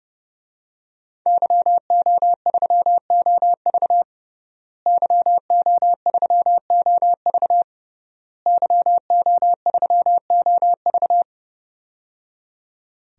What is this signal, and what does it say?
Morse code, heard as a keyed tone, about 30 wpm, 700 Hz. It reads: YO3OV YO3OV YO3OV
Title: DXCC-100 Call Signs - 5 Characters - Repeated 3x - 30wpm